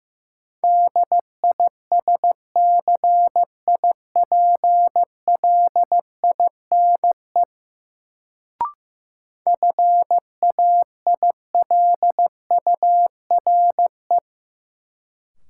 Morse code, heard as a keyed tone, about 15 wpm, 700 Hz. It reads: DISCIPLINE FAILURE
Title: Sentences from Top 900 Words - New Word Drill - 15wpm